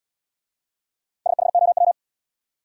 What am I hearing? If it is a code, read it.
Morse code, heard as a keyed tone, about 73 wpm, 700 Hz. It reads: SHOW